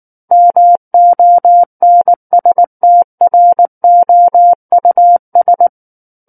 Morse code, heard as a keyed tone, about 19 wpm, 700 Hz. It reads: MONSTROUS
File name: Book One / 10-Chapter 10 - 19 WPM